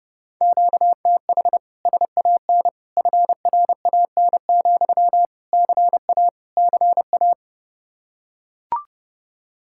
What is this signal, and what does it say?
Morse code, heard as a keyed tone, about 30 wpm, 700 Hz. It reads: QTH SAN FRAN, CA CA